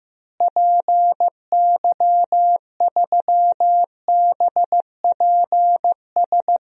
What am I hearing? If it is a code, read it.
Morse code, heard as a keyed tone, about 15 wpm, 700 Hz. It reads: PY3BPS
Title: DXCC-100 Call Signs - 6 Characters - Repeated 2x - 15wpm